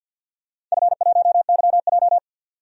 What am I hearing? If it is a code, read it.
Morse code, heard as a keyed tone, about 52 wpm, 700 Hz. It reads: F1YJ
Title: DXCC-100 Call Signs - 4 Characters - 50wpm